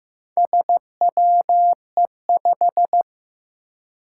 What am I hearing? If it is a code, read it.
Morse code, heard as a keyed tone, about 15 wpm, 700 Hz. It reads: SWE5
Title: Four Challenging Letter-Number Combinations 15wpm